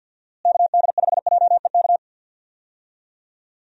Morse code, heard as a keyed tone, about 48 wpm, 700 Hz. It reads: KD5JEK
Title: USA Call Signs - 6 Characters - Repeated 3x - 50wpm